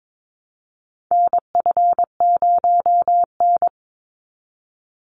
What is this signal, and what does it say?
Morse code, heard as a keyed tone, about 22 wpm, 700 Hz. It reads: NF0N
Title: Call Signs Repeated 3x - 22wpm